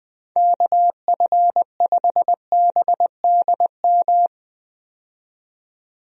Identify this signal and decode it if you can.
Morse code, heard as a keyed tone, about 21 wpm, 700 Hz. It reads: KF5BDM
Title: USA Call Signs - 6 Characters - 20wpm